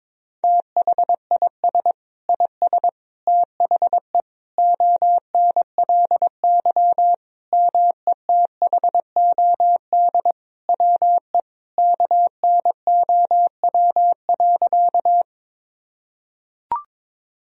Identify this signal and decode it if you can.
Morse code, heard as a keyed tone, about 22 wpm, 700 Hz. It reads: THIS IS THE ONLY METHOD WE KNOW.
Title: Sentences from Top 1000 Words - Review Part 3 - 22wpm